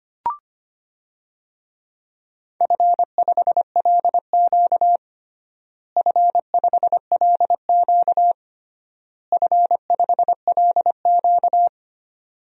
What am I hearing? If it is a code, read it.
Morse code, heard as a keyed tone, about 25 wpm, 700 Hz. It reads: F5LQ F5LQ F5LQ
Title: DXCC-100 Call Signs - 4 Characters - Repeated 3x - 25wpm